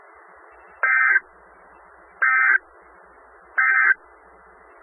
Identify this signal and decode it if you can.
Emergency Alert System (SAME) tones, heard as harsh buzzing bursts. It is heard from 0.8 to 3.9 s.
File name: RR Skywarn / Newton County WA4ASI Repeater